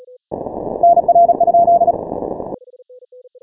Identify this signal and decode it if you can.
Morse code, heard as a keyed tone, about 36 wpm, 690 Hz. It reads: NR?
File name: final / W1UJ-168-140-MorseRunner